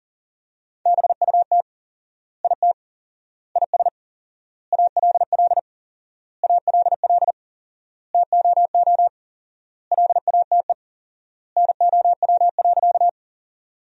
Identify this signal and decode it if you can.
Morse code, heard as a keyed tone, about 40 wpm, 700 Hz. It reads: BUT IT IS ALL ALL TOO LATE NOW.
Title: 4x Word-Spacing - Sentences from Top 300 Words - 40wpm